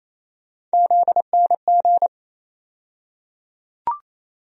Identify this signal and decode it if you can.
Morse code, heard as a keyed tone, about 28 wpm, 700 Hz. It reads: ZNG
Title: USA Call Sign Postfixes - 3 Letters 28wpm